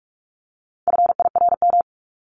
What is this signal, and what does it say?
Morse code, heard as a keyed tone, about 44 wpm, 700 Hz. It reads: FIRM
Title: Top 1251-1500 Words 45wpm